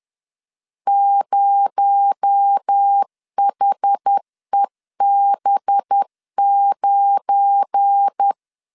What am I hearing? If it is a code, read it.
Morse code, heard as a keyed tone, about 10 wpm, 800 Hz. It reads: 0HEB9
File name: LEZIONI / 7 GF9Z / 7 GF9 10wpm 2747